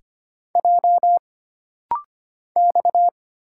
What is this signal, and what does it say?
Morse code, heard as a keyed tone, about 25 wpm, 700 Hz. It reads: J X
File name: Single Character (Letter:Number:Punctuation:Prosign) - Rapid-Fire - 25wpm